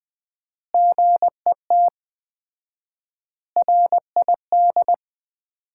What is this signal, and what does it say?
Morse code, heard as a keyed tone, about 20 wpm, 700 Hz. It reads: GET RID
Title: 4x Word-Spacing - Sets of 5 Words - 20wpm